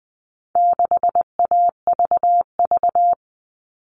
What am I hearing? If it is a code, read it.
Morse code, heard as a keyed tone, about 20 wpm, 700 Hz. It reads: 6AVV
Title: All Cut Numbers - 4-Digit 20wpm